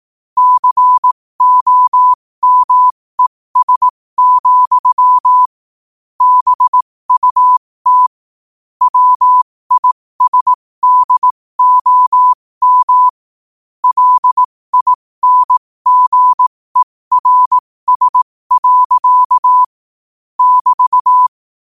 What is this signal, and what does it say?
Morse code, heard as a keyed tone, about 18 wpm, spaced as for 15 wpm, 1000 Hz. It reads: COMES, BUT WISDOM LINGERS. <BT>
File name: qotdfiles / 15wpm